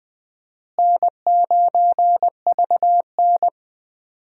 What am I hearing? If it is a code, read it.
Morse code, heard as a keyed tone, about 20 wpm, 700 Hz. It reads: N9VN
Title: USA Call Signs - 4 Characters - Repeated 3x - 20wpm